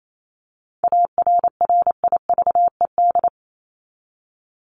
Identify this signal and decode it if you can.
Morse code, heard as a keyed tone, about 28 wpm, 700 Hz. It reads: ARRIVED
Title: ED Suffix - Top 100 Words with Base Words 28wpm